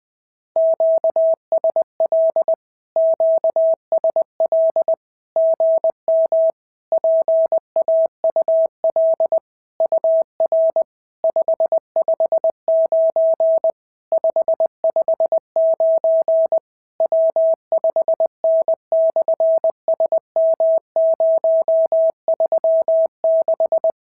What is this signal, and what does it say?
Morse code, heard as a keyed tone, about 20 wpm, 690 Hz. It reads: QSL QSL GM PAUL UR 559 559 W5N/SM036
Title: SOTA QSO Practice 20wpm